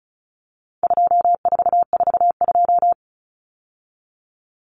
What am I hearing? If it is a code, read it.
Morse code, heard as a keyed tone, about 34 wpm, 700 Hz. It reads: 2442
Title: Numbers - 4-Digit 35wpm